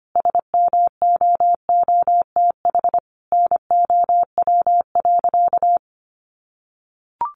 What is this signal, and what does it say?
Morse code, heard as a keyed tone, about 25 wpm, 700 Hz. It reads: SMOOTH NOW.
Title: Sentences from Top 1400 Words 25wpm